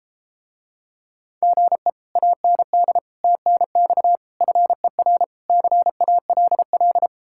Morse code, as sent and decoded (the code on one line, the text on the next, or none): --. / .- -. -.. / - -. -..- / ..-. .-. / -.-. .- .-.. .-..
GE AND TNX FER CALL